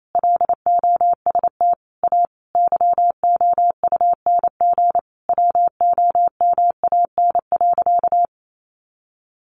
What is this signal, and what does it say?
Morse code, heard as a keyed tone, about 28 wpm, 700 Hz. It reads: LOST A YOUNG WOMAN.
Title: Sentences from Top 1400 Words 28wpm